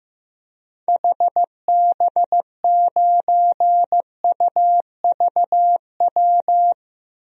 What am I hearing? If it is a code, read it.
Morse code, heard as a keyed tone, about 16 wpm, 700 Hz. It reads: HB9UVW
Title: DXCC-100 Call Signs - 6 Characters - Repeated 2x - 15wpm